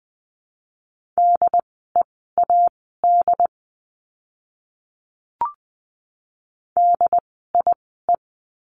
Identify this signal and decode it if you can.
Morse code, heard as a keyed tone, about 20 wpm, spaced as for 14 wpm, 700 Hz. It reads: DEAD DIE
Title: 075 Lesson 13f - Letter L - Words 20wpm